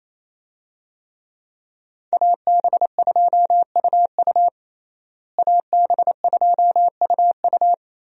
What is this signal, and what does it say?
Morse code, heard as a keyed tone, about 28 wpm, 700 Hz. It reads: AB2UU AB2UU
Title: USA Call Signs - 5 Characters - Repeated 3x - 28wpm